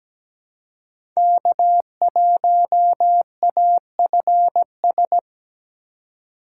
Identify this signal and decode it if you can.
Morse code, heard as a keyed tone, about 17 wpm, 700 Hz. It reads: K1AFS
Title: USA Call Signs - 5 Characters - 2x Character Spacing - 17wpm